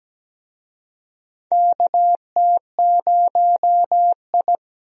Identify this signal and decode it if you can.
Morse code, heard as a keyed tone, about 17 wpm, 700 Hz. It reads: KT0I
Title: USA Call Signs - 4 Characters - Repeated 2x - 17wpm